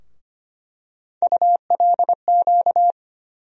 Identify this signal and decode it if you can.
Morse code, heard as a keyed tone, about 25 wpm, 700 Hz. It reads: ULQ
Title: USA Call Sign Postfixes - 3 Letters 25wpm